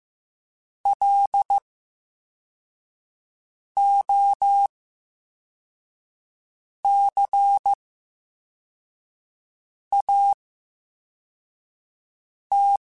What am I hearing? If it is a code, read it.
Morse code, heard as a keyed tone, about 15 wpm, spaced as for 4 wpm, 800 Hz. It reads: LOCAT
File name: Special INGs / Special INGs-mp3s / Slow ing @13